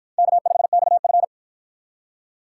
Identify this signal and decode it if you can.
Morse code, heard as a keyed tone, about 53 wpm, 700 Hz. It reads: K5XF